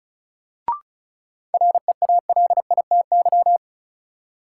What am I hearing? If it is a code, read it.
Morse code, heard as a keyed tone, about 36 wpm, 700 Hz. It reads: REALITY
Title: Sentences from Top 1200 Words - New Word Drill - 35wpm